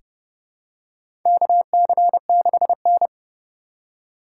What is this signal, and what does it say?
Morse code, heard as a keyed tone, about 32 wpm, 700 Hz. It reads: KC6N
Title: USA Call Signs - 2x Character Spacing - 30wpm